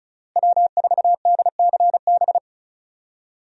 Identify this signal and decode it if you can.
Morse code, heard as a keyed tone, about 34 wpm, 700 Hz. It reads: W4DCB
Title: USA Call Signs - 5 Characters - Repeated 3x - 35wpm